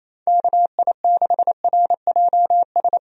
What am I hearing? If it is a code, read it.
Morse code, heard as a keyed tone, about 27 wpm, 700 Hz. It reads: KI6RJS
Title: USA Call Signs - 6 Characters - Repeated 2x - 28wpm